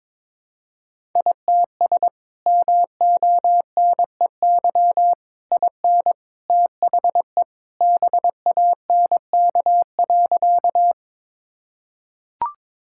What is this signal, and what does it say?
Morse code, heard as a keyed tone, about 22 wpm, 700 Hz. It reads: ITS MONEY IN THE BANK.
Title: Sentences from Top 1000 Words - Review Part 6 - 22wpm